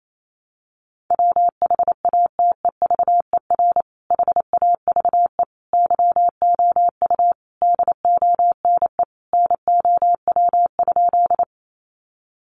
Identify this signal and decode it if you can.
Morse code, heard as a keyed tone, about 28 wpm, 700 Hz. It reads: WHATEVER HAVE YOU DONE NOW?
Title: Sentences from Top 1100 Words 28wpm